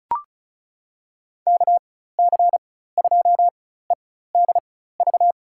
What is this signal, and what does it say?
Morse code, heard as a keyed tone, about 36 wpm, spaced as for 16 wpm, 700 Hz. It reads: KC2EDV
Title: USA Call Signs - 4x Character Spacing - 35wpm